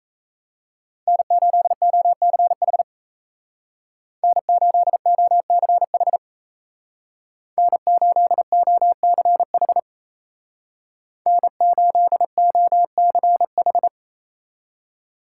Morse code, heard as a keyed tone, about 35 wpm, 700 Hz. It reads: N8OCH N8OCH N8OCH N8OCH
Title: Speed Racing - USA 1x3 Call Signs 28wpm